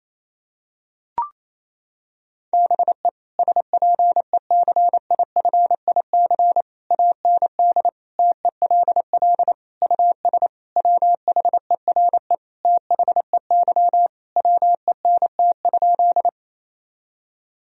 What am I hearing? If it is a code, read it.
Morse code, heard as a keyed tone, about 27 wpm, 700 Hz. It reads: BE SPECIFIC AND TELL US WHERE THEY WENT?
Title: Sentences from Top 1000 Words - Review Part 3 - 28wpm